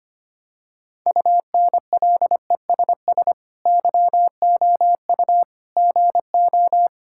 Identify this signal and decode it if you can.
Morse code, heard as a keyed tone, about 25 wpm, 700 Hz. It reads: UNLESS YOU GO
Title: Sentences from Top 1000 Words - Review Part 6 - 25wpm